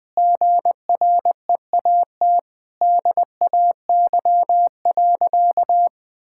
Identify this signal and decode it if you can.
Morse code, heard as a keyed tone, about 21 wpm, 700 Hz. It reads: GREAT DAY.
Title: Sentences with 6-Word Phrases - 20wpm